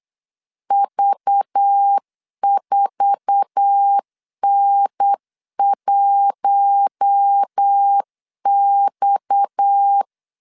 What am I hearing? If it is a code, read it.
Morse code, heard as a keyed tone, about 8 wpm, 800 Hz. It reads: V4N1X
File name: LEZIONI / 8 YJ45 / 8 YJ4 8wpm 3246